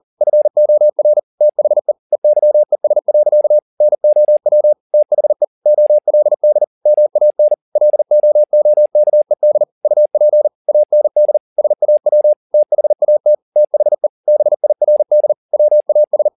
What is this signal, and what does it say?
Morse code, heard as a keyed tone, about 40 wpm, 600 Hz. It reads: FOR THE EYES. NOW THE OLD MAN LOOKED UP AND SAW THAT THE BIRD WAS